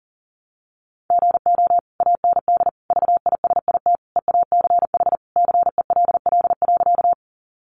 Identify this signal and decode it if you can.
Morse code, heard as a keyed tone, about 40 wpm, 700 Hz. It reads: GO AND VISIT EACH CELL.
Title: Sentences from Top 1100 Words 40wpm